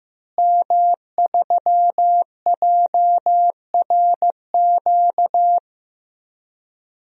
Morse code, heard as a keyed tone, about 16 wpm, 700 Hz. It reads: M3JRQ
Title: DXCC-100 Call Signs - 5 Characters - Repeated 2x - 15wpm